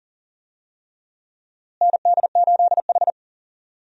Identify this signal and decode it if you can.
Morse code, heard as a keyed tone, about 40 wpm, 700 Hz. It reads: ND8H
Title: USA Call Signs - 4 Characters - Repeated 3x - 40wpm